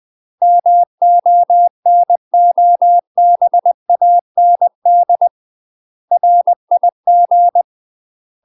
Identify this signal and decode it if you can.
Morse code, heard as a keyed tone, about 20 wpm, 700 Hz. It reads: MONOBAND RIG
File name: UK MASTER QSOs TRAINING/20WPM